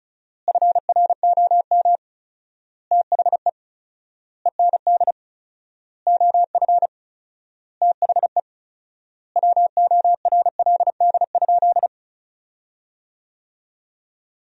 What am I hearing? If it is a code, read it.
Morse code, heard as a keyed tone, about 36 wpm, 700 Hz. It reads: FROM THE END OF THE WORLD?
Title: 4x Word-Spacing - Sentences from Top 200 Words - 35wpm